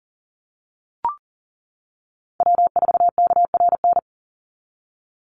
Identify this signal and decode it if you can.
Morse code, heard as a keyed tone, about 40 wpm, 700 Hz. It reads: W4KRN
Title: NA SOTA Chaser Callsigns 40wpm